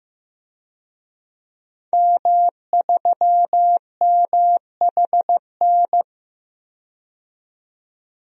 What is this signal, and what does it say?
Morse code, heard as a keyed tone, about 15 wpm, 700 Hz. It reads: M3MHN
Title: DXCC-100 Call Signs - 5 Characters - 15wpm